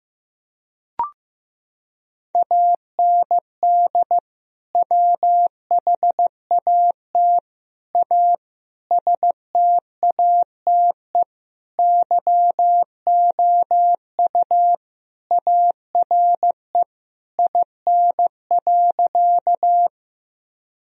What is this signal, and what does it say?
Morse code, heard as a keyed tone, about 15 wpm, 700 Hz. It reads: AND WHAT A STATE YOU ARE IN.
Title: Sentences from Top 1000 Words - Review Part 2 - 15wpm